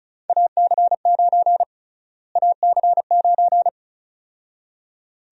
Morse code, heard as a keyed tone, about 36 wpm, 700 Hz. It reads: AC9 AC9
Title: USA Call Sign Prefixes - Repeated 2x - 35wpm